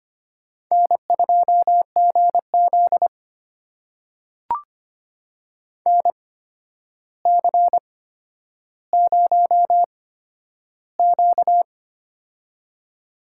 Morse code, heard as a keyed tone, about 25 wpm, spaced as for 11 wpm, 700 Hz. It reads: N2GZ NC0Q
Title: USA Call Signs - 4 Characters - 8x Character Spacing - 25wpm